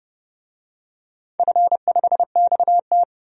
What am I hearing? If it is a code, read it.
Morse code, heard as a keyed tone, about 30 wpm, 700 Hz. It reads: F5XT
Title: DXCC-100 Call Signs - 4 Characters - Repeated 3x - 30wpm